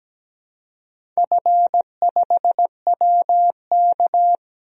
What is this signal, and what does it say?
Morse code, heard as a keyed tone, about 17 wpm, 700 Hz. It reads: F5WK
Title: DXCC-100 Call Signs - 4 Characters - Repeated 2x - 17wpm